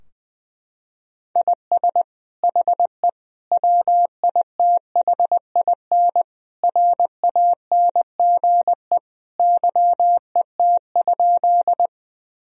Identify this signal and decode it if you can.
Morse code, heard as a keyed tone, about 20 wpm, 700 Hz. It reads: IS HE WITHIN RANGE YET?
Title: Sentences from Top 1000 Words - Review Part 4 - 20wpm